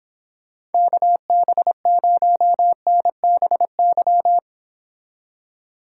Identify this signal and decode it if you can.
Morse code, heard as a keyed tone, about 29 wpm, 700 Hz. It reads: KB0NBY